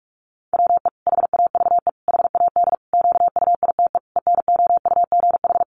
Morse code, heard as a keyed tone, about 44 wpm, 700 Hz. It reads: WE HAVE HAD QUITE ENOUGH
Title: Sentences from Top 700 Words 45wpm